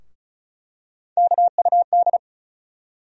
Morse code, heard as a keyed tone, about 36 wpm, 700 Hz. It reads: KUD
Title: USA Call Sign Postfixes - 3 Letters 35wpm